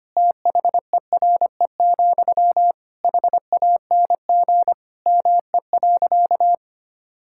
Morse code, heard as a keyed tone, about 25 wpm, 700 Hz. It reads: THERE, HANG ME.
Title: Sentences from Top 1600 Words 25wpm